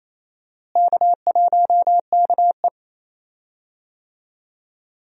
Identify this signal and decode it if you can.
Morse code, heard as a keyed tone, about 28 wpm, 700 Hz. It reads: K1KE
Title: USA Call Signs - 4 Characters - Repeated 3x - 28wpm